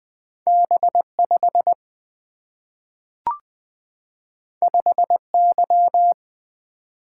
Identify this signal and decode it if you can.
Morse code, heard as a keyed tone, about 20 wpm, 700 Hz. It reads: B5 5Y
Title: Single-Letter Single-Number 20wpm